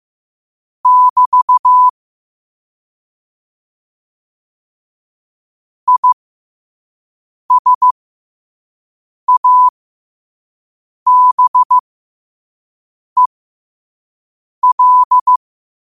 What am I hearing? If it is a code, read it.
Morse code, heard as a keyed tone, about 15 wpm, spaced as for 5 wpm, 1000 Hz. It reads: <BT> ISABEL
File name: qotdfiles / 5wpm